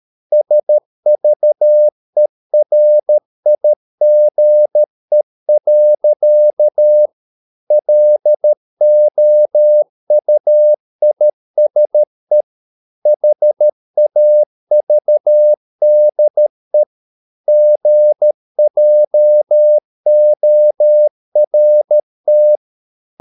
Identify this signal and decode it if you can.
Morse code, heard as a keyed tone, about 13 wpm, 600 Hz. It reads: SVERIGE. LOUISE HAVDE GJORT
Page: Ferie 13 wpm